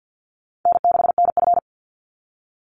Morse code, heard as a keyed tone, about 50 wpm, 700 Hz. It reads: N6NF
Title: Easy Call Signs - 50wpm